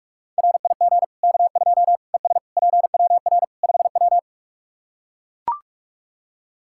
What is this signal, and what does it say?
Morse code, heard as a keyed tone, about 43 wpm, 700 Hz. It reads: RIG K2 ES PWR 5W
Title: Practice QSO - Exchange 2 - Info - 45wpm